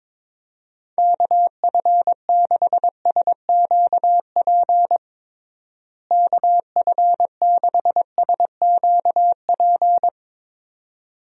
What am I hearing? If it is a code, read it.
Morse code, heard as a keyed tone, about 22 wpm, 700 Hz. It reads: KF6SQP KF6SQP
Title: USA Call Signs - 6 Characters - Repeated 2x - 22wpm